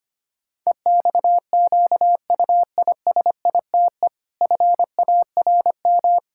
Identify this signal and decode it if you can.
Morse code, heard as a keyed tone, about 25 wpm, 700 Hz. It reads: EXQUISITE FARM
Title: Pangram Challenge v2 - 25wpm